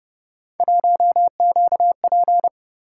Morse code, heard as a keyed tone, about 29 wpm, 700 Hz. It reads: 1QP